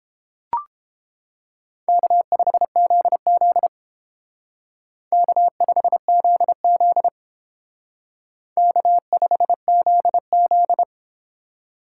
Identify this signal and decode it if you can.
Morse code, heard as a keyed tone, about 31 wpm, 700 Hz. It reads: K5ZZ K5ZZ K5ZZ